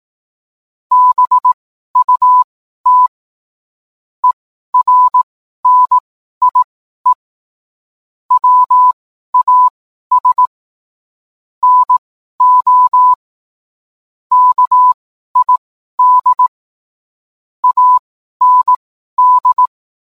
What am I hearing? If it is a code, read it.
Morse code, heard as a keyed tone, about 18 wpm, spaced as for 12 wpm, 1000 Hz. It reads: BUT ERNIE WAS NO KID AND